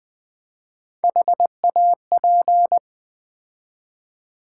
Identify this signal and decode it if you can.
Morse code, heard as a keyed tone, about 20 wpm, 700 Hz. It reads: HAP
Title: Build A Word - 3 Letters At A Time - 6-Letter Words - 20wpm